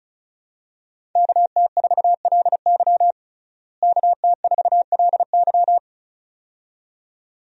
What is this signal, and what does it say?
Morse code, heard as a keyed tone, about 36 wpm, 700 Hz. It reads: KT4LY KT4LY
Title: USA Call Signs - 5 Characters - Repeated 2x - 35wpm